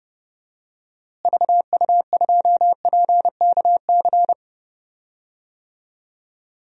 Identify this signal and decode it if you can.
Morse code, heard as a keyed tone, about 32 wpm, 700 Hz. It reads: VU2PKC
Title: DXCC-100 Call Signs - 6 Characters - Repeated 3x - 30wpm